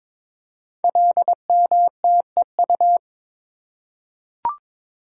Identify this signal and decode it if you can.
Morse code, heard as a keyed tone, about 22 wpm, 700 Hz. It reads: LMTEU
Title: Code Groups - 5 Letters - 22wpm